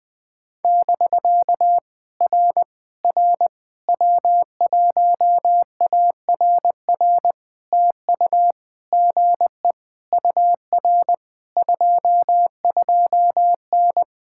Morse code, heard as a keyed tone, about 20 wpm, 700 Hz. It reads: <BK> R R W1ARR TU GE UR 22N
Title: POTA QSO Practice - 20wpm